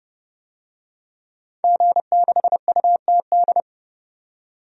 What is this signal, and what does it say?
Morse code, heard as a keyed tone, about 30 wpm, 700 Hz. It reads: G6UTD
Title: DXCC-100 Call Signs - 5 Characters - 30wpm